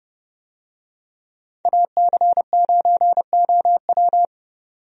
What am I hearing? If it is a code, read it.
Morse code, heard as a keyed tone, about 31 wpm, 700 Hz. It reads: AC9OW
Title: USA Call Signs - 5 Characters - 30wpm